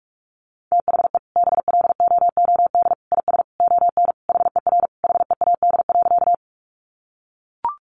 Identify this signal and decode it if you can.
Morse code, heard as a keyed tone, about 44 wpm, 700 Hz. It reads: THE BLOOD IS ON HER HEAD.
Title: Sentences from Top 1100 Words 45wpm